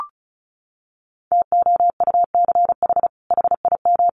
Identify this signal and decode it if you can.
Morse code, heard as a keyed tone, about 34 wpm, 700 Hz. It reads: TOUCH HIM
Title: Sentences from Top 1200 Words 35wpm